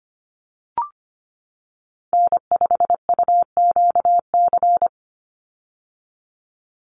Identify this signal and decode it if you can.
Morse code, heard as a keyed tone, about 25 wpm, 700 Hz. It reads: N5UQC
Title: USA Call Signs 25wpm